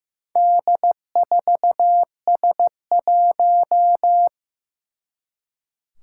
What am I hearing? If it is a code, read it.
Morse code, heard as a keyed tone, about 15 wpm, 700 Hz. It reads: D4S1
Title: Four Challenging Letter-Number Combinations 15wpm